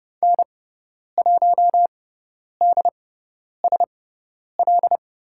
Morse code, heard as a keyed tone, about 30 wpm, spaced as for 10 wpm, 700 Hz. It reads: N1DSL
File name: Lesson 16g - Letter c - Call Signs 30wpm